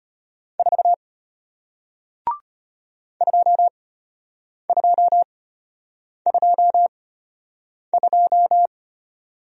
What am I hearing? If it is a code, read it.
Morse code, heard as a keyed tone, about 33 wpm, spaced as for 8 wpm, 700 Hz. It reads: 4 2222